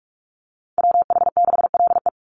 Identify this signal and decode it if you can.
Morse code, heard as a keyed tone, about 44 wpm, 700 Hz. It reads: WH6LE
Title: NA SOTA Chaser Callsigns 45wpm